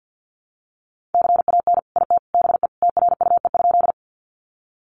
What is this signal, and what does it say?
Morse code, heard as a keyed tone, about 50 wpm, 700 Hz. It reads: CAN IT BE TRUE?
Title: Sentences from Top 400 Words 50wpm